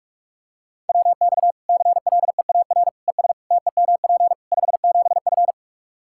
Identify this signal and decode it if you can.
Morse code, heard as a keyed tone, about 46 wpm, 700 Hz. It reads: WX CLEAR ES TEMP 57F
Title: Practice QSO - Exchange 2 - Info - 45wpm